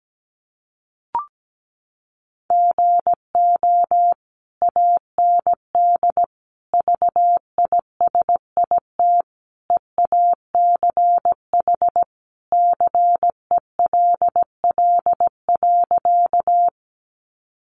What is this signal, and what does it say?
Morse code, heard as a keyed tone, about 17 wpm, 700 Hz. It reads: GO AND VISIT EACH CELL.
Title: Sentences from Top 1100 Words 17wpm